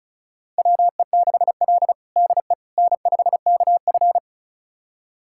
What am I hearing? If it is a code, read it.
Morse code, heard as a keyed tone, about 36 wpm, 700 Hz. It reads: WE6L DE N5KF
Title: USA Callsign DE Callsign - 4 Characters - 35wpm